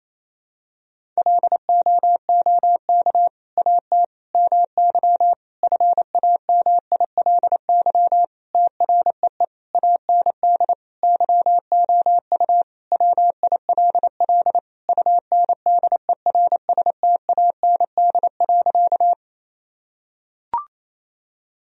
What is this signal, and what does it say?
Morse code, heard as a keyed tone, about 28 wpm, 700 Hz. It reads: LOOK AT MY FAMILY TREE AND YOU WILL UNDERSTAND.
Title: Sentences from Top 1000 Words - Review Part 1 - 28wpm